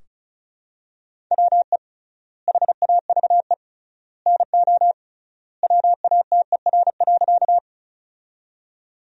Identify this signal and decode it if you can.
Morse code, heard as a keyed tone, about 36 wpm, 700 Hz. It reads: WE HAVE NO WATER.
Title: 3x Word-Spacing - Sentences from Top 100 Words - 35wpm